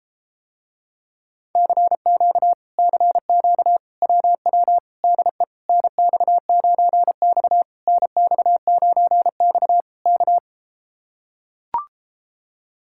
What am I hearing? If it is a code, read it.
Morse code, heard as a keyed tone, about 33 wpm, 700 Hz. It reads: CQ CQ WW DE NX9X NX9X K